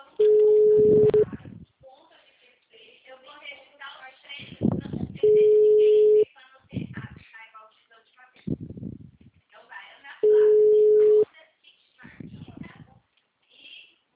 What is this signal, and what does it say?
Ringback tone (Europe, 425 Hz), with speech.
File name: out- -2013- - - .31554